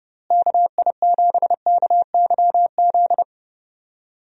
Morse code, heard as a keyed tone, about 31 wpm, 700 Hz. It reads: KI7KYZ
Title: USA Call Signs - 6 Characters - Repeated 2x - 30wpm